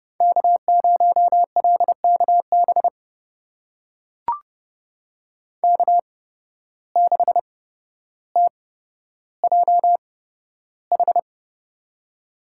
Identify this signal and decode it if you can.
Morse code, heard as a keyed tone, about 30 wpm, spaced as for 13 wpm, 700 Hz. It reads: K0LKB K6TJH
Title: USA Call Signs - 8x Character Spacing - 30wpm